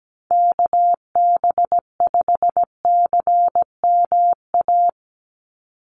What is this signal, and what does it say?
Morse code, heard as a keyed tone, about 17 wpm, 700 Hz. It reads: KB5CMA
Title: USA Call Signs 17wpm